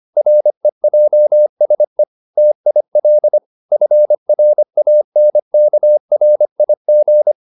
Morse code, heard as a keyed tone, about 25 wpm, 600 Hz. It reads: REJSE TIL FRANKRIG
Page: Ferie 25 wpm